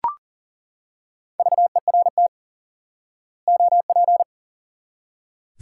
Morse code, heard as a keyed tone, about 40 wpm, 700 Hz. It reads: VERT OP
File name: IQE Course 3 - Lesson 89 - YAGI - Two Elements - 40wpm